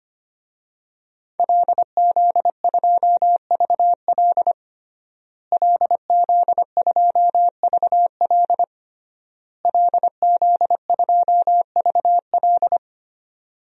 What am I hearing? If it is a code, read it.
Morse code, heard as a keyed tone, about 25 wpm, 700 Hz. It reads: LZ2VL LZ2VL LZ2VL
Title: DXCC-100 Call Signs - 5 Characters - Repeated 3x - 25wpm